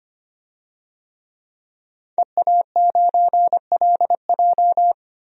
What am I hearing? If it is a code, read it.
Morse code, heard as a keyed tone, about 25 wpm, 700 Hz. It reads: EA9LJ
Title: DXCC-100 Call Signs - 5 Characters - 25wpm